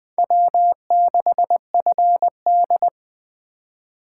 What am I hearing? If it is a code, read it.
Morse code, heard as a keyed tone, about 21 wpm, 700 Hz. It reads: W6FD
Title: Speed Racing - USA Call Signs 17wpm